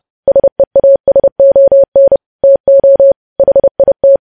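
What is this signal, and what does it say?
Morse code, heard as a keyed tone, about 30 wpm, 550 Hz. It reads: SEASON TO HIT
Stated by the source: Morse code practice of news headlines at 30 wpm, preceded by vvv.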